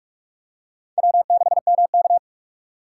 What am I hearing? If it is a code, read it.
Morse code, heard as a keyed tone, about 46 wpm, 700 Hz. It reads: W6MK
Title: USA Call Signs - 4 Characters - 3x Character Spacing - 45wpm